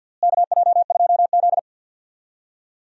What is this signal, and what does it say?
Morse code, heard as a keyed tone, about 48 wpm, 700 Hz. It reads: KJ2Z
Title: USA Call Signs - 4 Characters - Repeated 2x - 50wpm